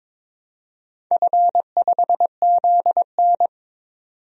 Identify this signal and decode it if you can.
Morse code, heard as a keyed tone, about 22 wpm, 700 Hz. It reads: F5ZN
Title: DXCC-100 Call Signs - 4 Characters - 22wpm